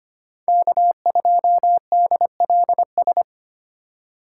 Morse code, heard as a keyed tone, about 30 wpm, 700 Hz. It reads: K2DLS
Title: Speed Racing - USA 1x3 Call Signs 25wpm